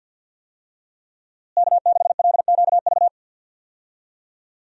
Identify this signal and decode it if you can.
Morse code, heard as a keyed tone, about 50 wpm, 700 Hz. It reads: K6LQV
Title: USA Call Signs - 5 Characters - Repeated 2x - 50wpm